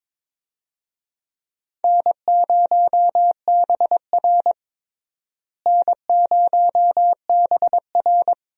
Morse code, heard as a keyed tone, about 22 wpm, 700 Hz. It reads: N0BR N0BR
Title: USA Call Signs - 4 Characters - Repeated 3x - 22wpm